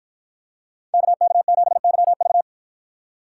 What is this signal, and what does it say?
Morse code, heard as a keyed tone, about 53 wpm, 700 Hz. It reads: KK7YV